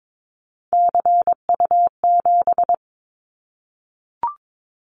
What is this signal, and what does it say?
Morse code, heard as a keyed tone, about 22 wpm, 700 Hz. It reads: CU7
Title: Callsign Prefixes for Top 50 DXCC - 22wpm